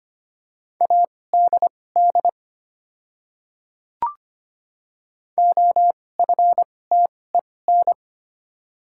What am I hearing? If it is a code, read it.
Morse code, heard as a keyed tone, about 25 wpm, spaced as for 18 wpm, 700 Hz. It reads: ADD OFTEN
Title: 2x Character Spacing - Top 300 Words - 25wpm